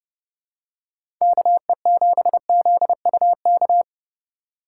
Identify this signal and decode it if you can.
Morse code, heard as a keyed tone, about 30 wpm, 700 Hz. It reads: KE7ZUK
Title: USA Call Signs - 6 Characters - 3x Character Spacing - 30wpm